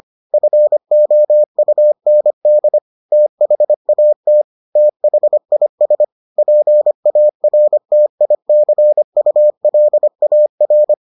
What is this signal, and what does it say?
Morse code, heard as a keyed tone, about 25 wpm, 600 Hz. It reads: FOUND THAT THIS PARTICULAR